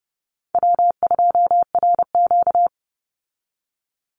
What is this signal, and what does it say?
Morse code, heard as a keyed tone, about 30 wpm, 700 Hz. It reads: W2RQ
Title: Call Signs Repeated 2x - 30wpm